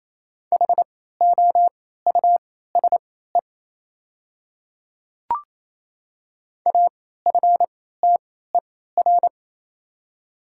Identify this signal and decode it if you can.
Morse code, heard as a keyed tone, about 28 wpm, spaced as for 16 wpm, 700 Hz. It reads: HOUSE AFTER
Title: 3x Character Spacing - Top 200 Words - 28wpm